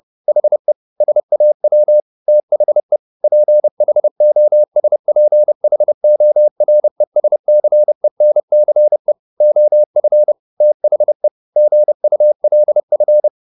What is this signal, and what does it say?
Morse code, heard as a keyed tone, about 30 wpm, 600 Hz. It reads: HE SAW THE PHOSPHORESCENCE OF THE GULF
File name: the old man and the sea / the old man and the sea 1 30wpm